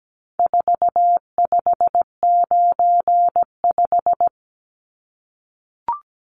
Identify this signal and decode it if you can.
Morse code, heard as a keyed tone, about 17 wpm, 700 Hz. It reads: 4595
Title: Numbers - 4-Digit 17wpm